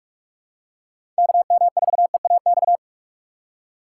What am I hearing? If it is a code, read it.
Morse code, heard as a keyed tone, about 45 wpm, 700 Hz. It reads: KM4EAX
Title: USA Call Signs - 6 Characters - 2x Character Spacing - 45wpm